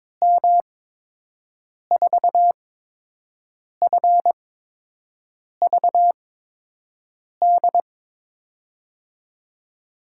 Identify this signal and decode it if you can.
Morse code, heard as a keyed tone, about 22 wpm, spaced as for 6 wpm, 700 Hz. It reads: M4FVD
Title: USA Call Signs - 6 Characters - 8x Character Spacing - 22wpm